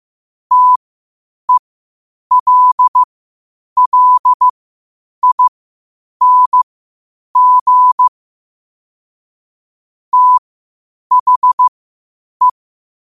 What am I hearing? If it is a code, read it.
Morse code, heard as a keyed tone, about 15 wpm, spaced as for 8 wpm, 1000 Hz. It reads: TELLING THE
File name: qotdfiles / 8wpm